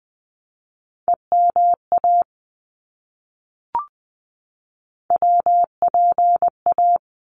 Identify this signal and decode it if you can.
Morse code, heard as a keyed tone, about 20 wpm, 700 Hz. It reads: EMA WPA
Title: ARRL Sections 20wpm